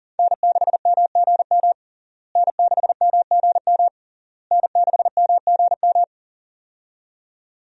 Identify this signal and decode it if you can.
Morse code, heard as a keyed tone, about 40 wpm, 700 Hz. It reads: N6MGM N6MGM N6MGM
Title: USA Call Signs - 5 Characters - Repeated 3x - 40wpm